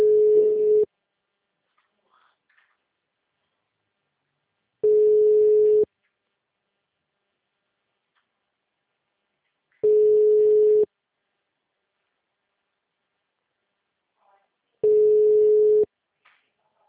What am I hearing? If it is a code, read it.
Ringback tone (Europe, 425 Hz).
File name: out- -2016- - - .32262